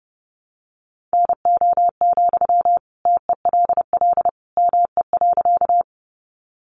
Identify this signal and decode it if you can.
Morse code, heard as a keyed tone, about 30 wpm, 700 Hz. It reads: NO, TELL ME.
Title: Sentences from Top 200 Words 30wpm